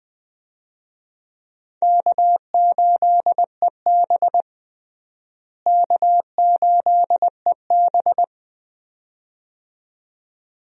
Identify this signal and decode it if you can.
Morse code, heard as a keyed tone, about 20 wpm, 700 Hz. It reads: K8EB K8EB
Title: USA Call Signs - 4 Characters - Repeated 2x - 20wpm